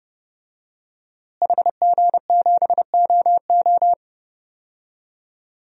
Morse code, heard as a keyed tone, about 32 wpm, 700 Hz. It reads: HG7OO
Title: DXCC-100 Call Signs - 5 Characters - Repeated 3x - 30wpm